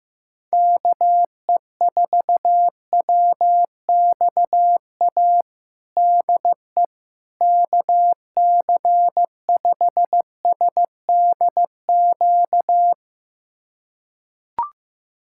Morse code, heard as a keyed tone, about 15 wpm, 700 Hz. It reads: KE4WXA DE KC5SDQ
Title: USA Callsign DE Callsign - 6 Characters - 15wpm